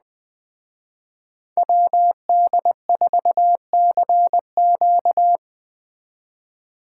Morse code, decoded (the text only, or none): WD4CQ